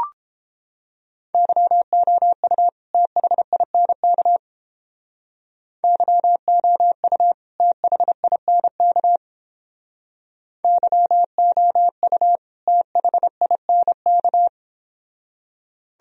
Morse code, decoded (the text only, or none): YOU THINK YOU THINK YOU THINK